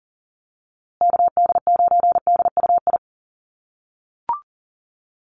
Morse code, heard as a keyed tone, about 40 wpm, 700 Hz. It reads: KD9DUI
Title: USA Call Signs Repeated 3x 40wpm